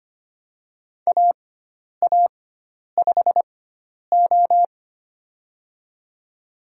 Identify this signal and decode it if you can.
Morse code, heard as a keyed tone, about 25 wpm, spaced as for 10 wpm, 700 Hz. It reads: AA5O
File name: Lesson 16g - Letter c - Call Signs 25wpm